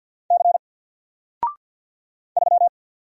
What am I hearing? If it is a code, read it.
Morse code, heard as a keyed tone, about 50 wpm, 700 Hz. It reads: C 3
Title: Single Character (Letter:Number:Punctuation:Prosign) 50wpm